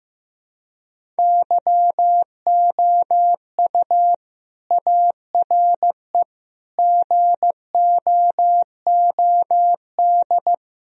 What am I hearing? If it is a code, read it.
Morse code, heard as a keyed tone, about 15 wpm, 700 Hz. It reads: YOU ARE GOOD
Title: Sentences from Top 1000 Words - Review Part 1 - 15wpm